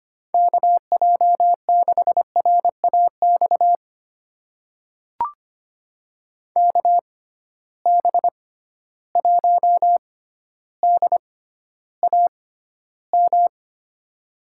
Morse code, heard as a keyed tone, about 25 wpm, spaced as for 13 wpm, 700 Hz. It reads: KJ6RAX KB1DAM
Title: USA Call Signs - 6 Characters - 6x Character Spacing - 25wpm